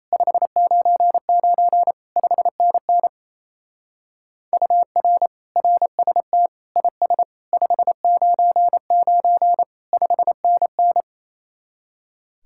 Morse code, heard as a keyed tone, about 32 wpm, 700 Hz. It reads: 599 5NN UR RST IS 599 5NN
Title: Speed-Racing Common QSO Phrases Encore - 28wpm